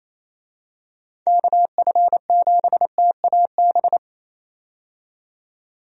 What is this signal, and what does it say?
Morse code, heard as a keyed tone, about 28 wpm, 700 Hz. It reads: KF7TAB
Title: USA Call Signs - 6 Characters - 28wpm